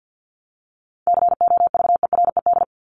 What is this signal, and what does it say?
Morse code, heard as a keyed tone, about 50 wpm, 700 Hz. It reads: COVERED
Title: ED Suffix - Top 100 Words with Base Words 50wpm